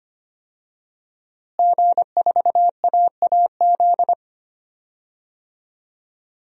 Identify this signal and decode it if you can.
Morse code, heard as a keyed tone, about 25 wpm, 700 Hz. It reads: G4AAZ
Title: DXCC-100 Call Signs - 5 Characters - Repeated 2x - 25wpm